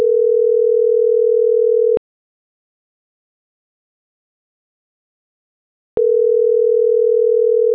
Ringback tone (North America), 440 + 480 Hz.
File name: CENTRAL TERRESTRE / out karis